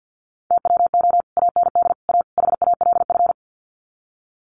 Morse code, heard as a keyed tone, about 50 wpm, 700 Hz. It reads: TWO AND A HALF
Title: Sets of 4 Words 50wpm